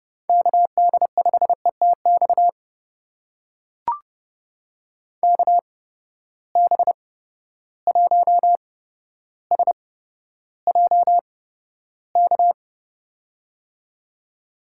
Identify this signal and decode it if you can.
Morse code, heard as a keyed tone, about 31 wpm, spaced as for 13 wpm, 700 Hz. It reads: KD5ETX KB1SJK
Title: USA Call Signs - 6 Characters - 8x Character Spacing - 30wpm